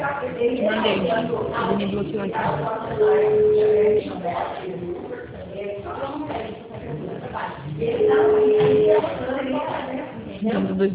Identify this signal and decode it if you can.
Ringback tone (Europe, 425 Hz), with speech.